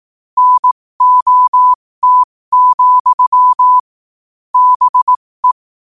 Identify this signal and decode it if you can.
Morse code, heard as a keyed tone, about 18 wpm, spaced as for 15 wpm, 1000 Hz. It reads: NOT, BE
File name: qotdfiles / 15wpm